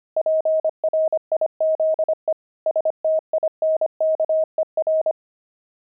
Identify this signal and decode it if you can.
Morse code, heard as a keyed tone, about 25 wpm, 630 Hz. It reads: PRIZE STINKER
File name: pangram-fs-25-cs-25-jolt